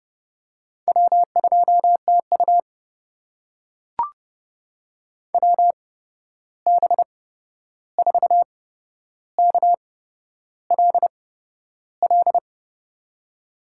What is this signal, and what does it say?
Morse code, heard as a keyed tone, about 30 wpm, spaced as for 11 wpm, 700 Hz. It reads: W2TU WB4KLL
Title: USA Call Signs - 8x Character Spacing - 30wpm